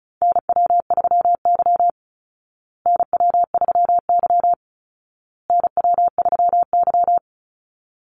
Morse code, heard as a keyed tone, about 34 wpm, 700 Hz. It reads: NW3Y NW3Y NW3Y
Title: Call Signs Repeated 2x - 35wpm